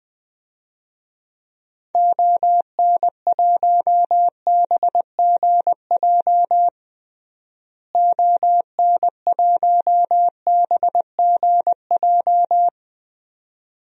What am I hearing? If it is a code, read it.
Morse code, heard as a keyed tone, about 21 wpm, 700 Hz. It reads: ON1BGJ ON1BGJ
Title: DXCC-100 Call Signs - 6 Characters - Repeated 2x - 20wpm